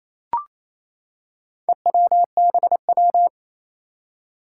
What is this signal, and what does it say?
Morse code, heard as a keyed tone, about 28 wpm, 700 Hz. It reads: EWBW
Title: Four Challenging Letter-Number Combinations 28wpm